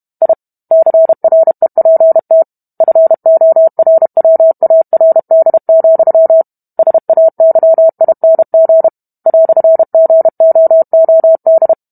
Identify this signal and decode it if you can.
Morse code, heard as a keyed tone, about 32 wpm, 650 Hz. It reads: I CREPT FORWARD, SAYING "GOOD